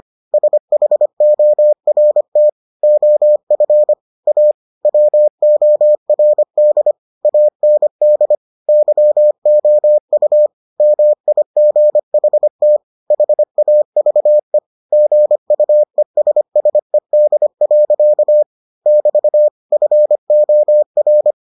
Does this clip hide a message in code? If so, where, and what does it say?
Yes: Morse code from 0.3 to 21.3 s, heard as a keyed tone, about 25 wpm, 600 Hz. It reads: SHORT OF A WORD AND YOU MIGHT HAVE GUESSED. <BT> FOR